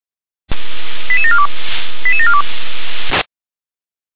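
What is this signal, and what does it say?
Five-tone selective call (ZVEI), heard as quick runs of five tones, tones of about 70 ms. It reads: 88742 88742
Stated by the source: ZVEI five-tone selective call received on SDR.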